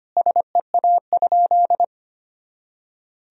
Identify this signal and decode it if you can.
Morse code, heard as a keyed tone, about 25 wpm, 700 Hz. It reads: SEA?
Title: Sentences from Top 300 Words and Abbreviations 25wpm